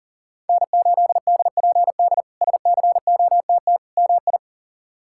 Morse code, heard as a keyed tone, about 40 wpm, 700 Hz. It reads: N8DPD SCOTT MI
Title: K1USN SST QSO Practice - 40wpm